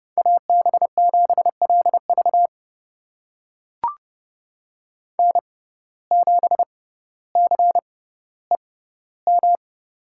Morse code, heard as a keyed tone, about 30 wpm, spaced as for 15 wpm, 700 Hz. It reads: AB7LV N7CEM
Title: USA Call Signs - 5 Characters - 6x Character Spacing - 30wpm